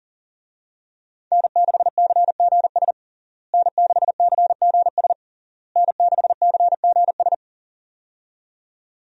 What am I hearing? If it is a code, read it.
Morse code, heard as a keyed tone, about 40 wpm, 700 Hz. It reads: N6CGS N6CGS N6CGS
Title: USA Call Signs - 5 Characters - Repeated 3x - 40wpm